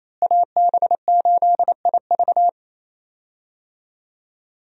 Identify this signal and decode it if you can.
Morse code, heard as a keyed tone, about 28 wpm, 700 Hz. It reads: AB8IV
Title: USA Call Signs - 5 Characters - Repeated 3x - 28wpm